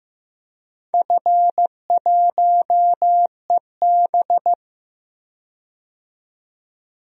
Morse code, heard as a keyed tone, about 15 wpm, 700 Hz. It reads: F1EB
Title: DXCC-100 Call Signs - 4 Characters - 15wpm